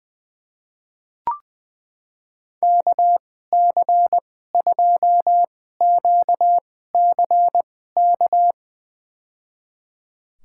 Morse code, heard as a keyed tone, about 20 wpm, spaced as for 14 wpm, 700 Hz. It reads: KC2QCK
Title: USA Call Signs - 2x Character Spacing - 20wpm